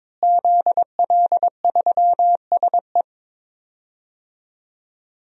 Morse code, heard as a keyed tone, about 21 wpm, 700 Hz. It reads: ZL3SE
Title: DXCC-100 Call Signs - 5 Characters - Repeated 2x - 22wpm